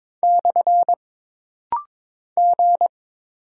Morse code, heard as a keyed tone, about 22 wpm, 700 Hz. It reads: / G
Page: This rapid-fire timing is geared to push your ICR (Instant Character Recognition) skills.